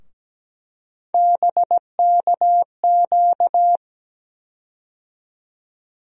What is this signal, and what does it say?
Morse code, heard as a keyed tone, about 17 wpm, 700 Hz. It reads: BKQ
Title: USA Call Sign Postfixes - 3 Letters 17wpm